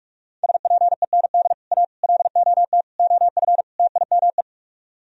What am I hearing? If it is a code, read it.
Morse code, heard as a keyed tone, about 40 wpm, 700 Hz. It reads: SPEND A LOT OF TIME